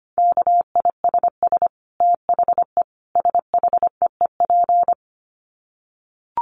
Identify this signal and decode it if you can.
Morse code, heard as a keyed tone, about 25 wpm, 700 Hz. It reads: KISS THE SHEEP
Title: Dit Challenge 25wpm